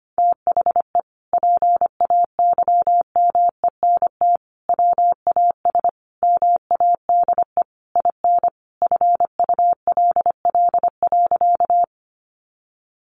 Morse code, heard as a keyed tone, about 25 wpm, 700 Hz. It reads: THE PAYMENT WAS MADE IN FULL.
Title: Sentences from Top 1200 Words 25wpm